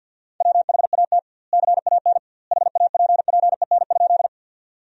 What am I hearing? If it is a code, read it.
Morse code, heard as a keyed tone, about 50 wpm, 700 Hz. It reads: WHAT CAN HAPPEN?
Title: 2x Word-Spacing - Sentences from Top 400 Words - 50wpm